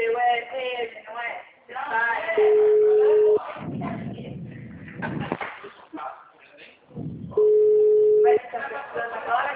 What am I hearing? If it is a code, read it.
Ringback tone (Europe, 425 Hz), with speech.